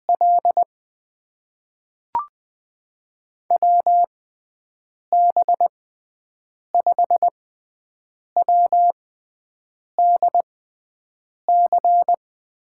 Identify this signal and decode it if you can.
Morse code, heard as a keyed tone, about 20 wpm, spaced as for 7 wpm, 700 Hz. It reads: L WB5WDC
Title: USA Call Signs - 6 Characters - 6x Character Spacing - 20wpm